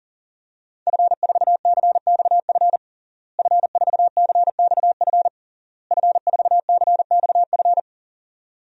Morse code, heard as a keyed tone, about 40 wpm, 700 Hz. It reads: F4CXF F4CXF F4CXF
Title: DXCC-100 Call Signs - 5 Characters - Repeated 3x - 40wpm